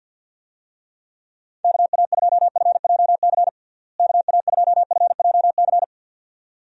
Morse code, heard as a keyed tone, about 50 wpm, 700 Hz. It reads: KA2FJC KA2FJC
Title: USA Call Signs - 6 Characters - Repeated 2x - 50wpm